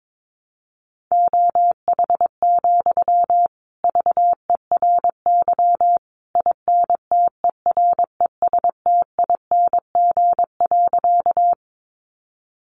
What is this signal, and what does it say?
Morse code, heard as a keyed tone, about 22 wpm, 700 Hz. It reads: OH, VERY INTERESTING.
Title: Sentences from Top 1000 Words 22wpm